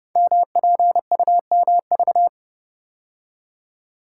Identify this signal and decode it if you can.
Morse code, heard as a keyed tone, about 30 wpm, 700 Hz. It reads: MPUMV
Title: Code Groups - 5 Letters - 30wpm